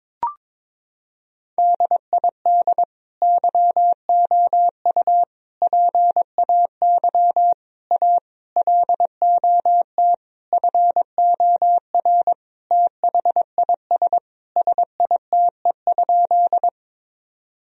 Morse code, heard as a keyed tone, about 22 wpm, 700 Hz. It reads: DID YOU PAY A LOT FOR THIS SITE?
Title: Sentences from Top 1000 Words - Review Part 4 - 22wpm